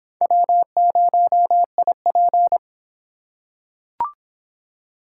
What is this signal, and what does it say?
Morse code, heard as a keyed tone, about 26 wpm, 700 Hz. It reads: W0IP